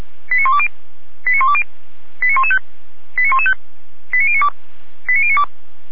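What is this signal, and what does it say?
Five-tone selective call (ZVEI), heard as quick runs of five tones, tones of about 70 ms. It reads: 89120 89120 89115 89115 89092 89092
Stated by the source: Five-tone ZVEI selective call received on an SDR.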